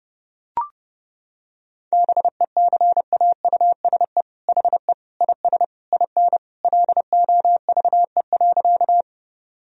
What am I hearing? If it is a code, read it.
Morse code, heard as a keyed tone, about 30 wpm, 700 Hz. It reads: BECAUSE HE IS IN LOVE.
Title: Sentences from Top 1000 Words - Review Part 3 - 30wpm